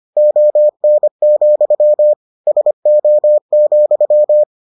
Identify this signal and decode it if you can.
Morse code, heard as a keyed tone, about 25 wpm, 600 Hz. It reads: ON, SO,